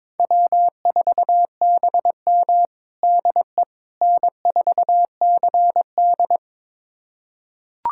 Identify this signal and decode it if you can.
Morse code, heard as a keyed tone, about 21 wpm, 700 Hz. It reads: W4BM DE N4CD
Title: USA Callsign DE Callsign - 4 Characters - 22wpm